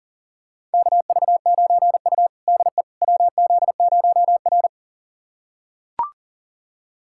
Morse code, heard as a keyed tone, about 40 wpm, 700 Hz. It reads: KV9U DE WZ0R
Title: USA Callsign DE Callsign - 4 Characters - 40wpm